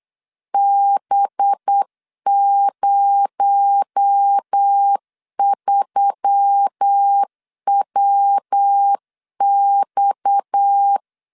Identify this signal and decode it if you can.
Morse code, heard as a keyed tone, about 8 wpm, 800 Hz. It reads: B03WX
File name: LEZIONI / 6 PKX0 / 6 PKX0 8wpm 3341